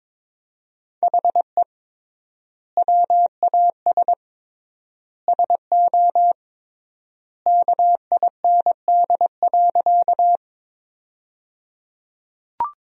Morse code, heard as a keyed tone, about 22 wpm, 700 Hz. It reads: HE WAS SO KIND.
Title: 3x Word-Spacing - Sentences from Top 200 Words - 22wpm